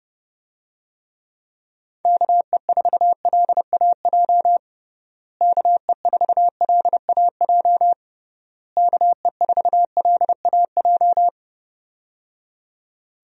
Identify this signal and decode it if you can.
Morse code, heard as a keyed tone, about 30 wpm, 700 Hz. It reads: KE4LAJ KE4LAJ KE4LAJ
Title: USA Call Signs - 6 Characters - Repeated 3x - 30wpm